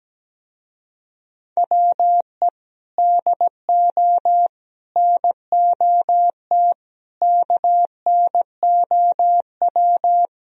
Morse code, heard as a keyed tone, about 17 wpm, 700 Hz. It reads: WE DO NOT KNOW
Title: Speed-Racing - Sets of 4 Words 17wpm